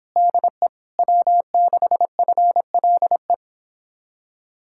Morse code, heard as a keyed tone, about 31 wpm, 700 Hz. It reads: DE W6FLE